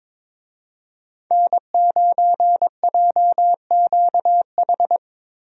Morse code, heard as a keyed tone, about 22 wpm, 700 Hz. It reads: N9JQH
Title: USA Call Signs - 5 Characters - Repeated 2x - 22wpm